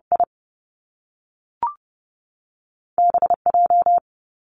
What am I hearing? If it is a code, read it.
Morse code, heard as a keyed tone, about 30 wpm, 700 Hz. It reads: I B J
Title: Top Two Letter Combinations in Words 30wpm